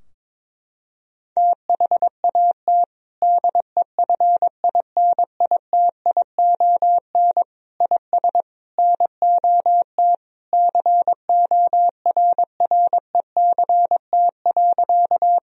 Morse code, heard as a keyed tone, about 22 wpm, 700 Hz. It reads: THAT DEFINITION IS NOT CORRECT.
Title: Sentences from Top 1000 Words - Review Part 4 - 22wpm